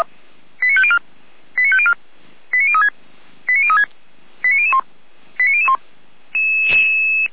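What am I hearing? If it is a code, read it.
Five-tone selective call (ZVEI), heard as quick runs of five tones, tones of about 70 ms. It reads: 89504 89504 89036 89036 89001 89001
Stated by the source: Five-tone ZVEI selective call received via SDR, from fire brigade Feuerwehr Orlamünde's audio server.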